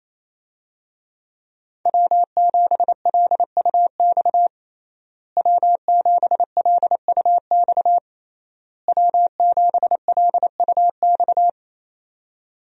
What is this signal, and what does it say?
Morse code, heard as a keyed tone, about 28 wpm, 700 Hz. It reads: W7LUX W7LUX W7LUX
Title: USA Call Signs - 5 Characters - Repeated 3x - 28wpm